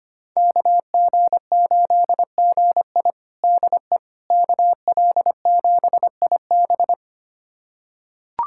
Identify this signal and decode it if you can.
Morse code, heard as a keyed tone, about 25 wpm, 700 Hz. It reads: KG8GI DE KL7IB
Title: USA Callsign DE Callsign 25wpm